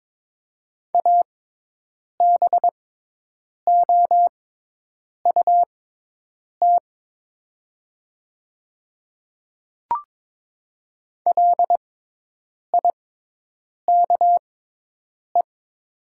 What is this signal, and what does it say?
Morse code, heard as a keyed tone, about 21 wpm, spaced as for 8 wpm, 700 Hz. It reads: ABOUT LIKE
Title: 6x Character Spacing - Top 100 Words - 22wpm